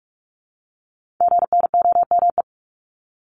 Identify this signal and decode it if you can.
Morse code, heard as a keyed tone, about 46 wpm, 690 Hz. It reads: GNOME
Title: Multi-letter Phonemes Repeated 3x 45wpm